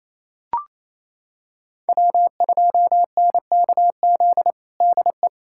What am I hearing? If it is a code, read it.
Morse code, heard as a keyed tone, about 28 wpm, 700 Hz. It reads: W2NKZ DE
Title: USA Callsign DE Callsign 28wpm